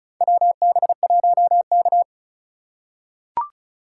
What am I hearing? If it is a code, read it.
Morse code, heard as a keyed tone, about 34 wpm, 700 Hz. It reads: WB1K
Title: USA Call Signs - 4x Character Spacing - 35wpm